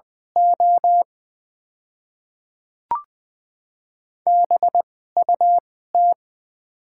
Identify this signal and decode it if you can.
Morse code, heard as a keyed tone, about 20 wpm, spaced as for 15 wpm, 700 Hz. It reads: O BUT
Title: 2x Character Spacing - Top 100 Words - 20wpm